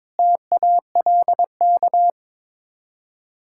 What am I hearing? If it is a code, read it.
Morse code, heard as a keyed tone, about 22 wpm, 700 Hz. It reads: TALK
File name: practice / 22wpm / Build A Phrase - 5-Word Phrases - 22wpm